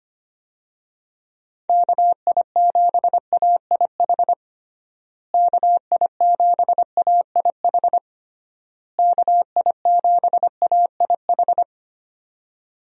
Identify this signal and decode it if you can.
Morse code, heard as a keyed tone, about 25 wpm, 700 Hz. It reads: KI7AIH KI7AIH KI7AIH
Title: USA Call Signs - 6 Characters - Repeated 3x - 25wpm